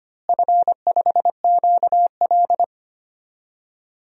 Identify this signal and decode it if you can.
Morse code, heard as a keyed tone, about 25 wpm, 700 Hz. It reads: F5QL
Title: DXCC-100 Call Signs - 4 Characters - Repeated 3x - 25wpm